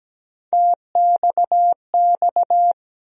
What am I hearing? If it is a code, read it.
Morse code, heard as a keyed tone, about 17 wpm, 700 Hz. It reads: TXX
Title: USA Call Sign Postfixes - 3 Letters 17wpm